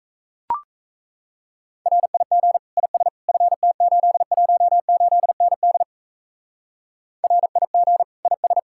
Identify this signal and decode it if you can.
Morse code, heard as a keyed tone, about 34 wpm, 700 Hz. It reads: RIG IS FT818ND RIG IS
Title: Speed-Racing Common QSO Phrases - 28wpm